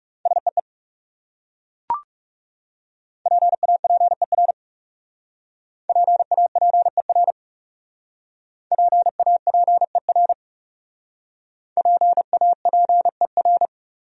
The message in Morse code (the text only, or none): SEE PAPER PAPER PAPER PAPER